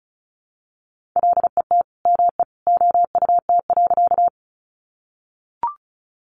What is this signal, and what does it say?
Morse code, heard as a keyed tone, about 34 wpm, 700 Hz. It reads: LET ME OUT.
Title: Sentences from Top 300 Words 35wpm